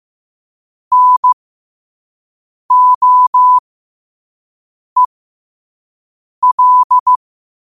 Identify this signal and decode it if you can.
Morse code, heard as a keyed tone, about 15 wpm, spaced as for 5 wpm, 1000 Hz. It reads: NOEL
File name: qotdfiles / 5wpm